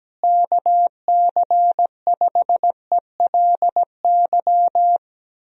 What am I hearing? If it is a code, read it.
Morse code, heard as a keyed tone, about 17 wpm, 700 Hz. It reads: KC5ELY
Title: USA Call Signs - 6 Characters - Repeated 2x - 17wpm